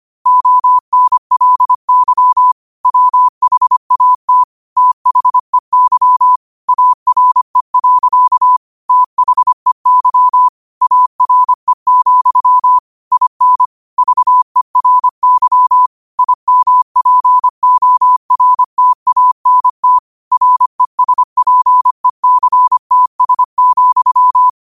Morse code, heard as a keyed tone, about 25 wpm, 1000 Hz. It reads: ONLY WHAT THEY ARE. THEY ARE, IN VERY IMPORTANT RESPECTS,
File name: qotdfiles / 25wpm